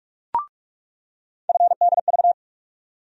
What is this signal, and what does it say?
Morse code, heard as a keyed tone, about 46 wpm, 700 Hz. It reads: FDV
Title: USA Call Sign Postfixes - 3 Letters 45wpm